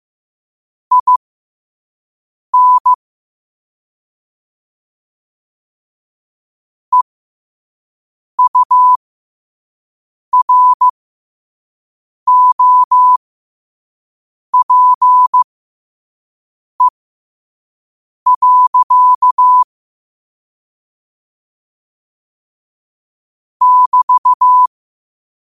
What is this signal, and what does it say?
Morse code, heard as a keyed tone, about 15 wpm, spaced as for 5 wpm, 1000 Hz. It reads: IN EUROPE. <BT>
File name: qotdfiles / 5wpm